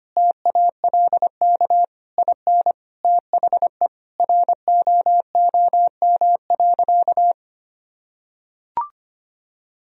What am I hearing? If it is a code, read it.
Morse code, heard as a keyed tone, about 25 wpm, 700 Hz. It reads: TALK IN THE ROOM.
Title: Sentences from Top 1000 Words - Review Part 4 - 25wpm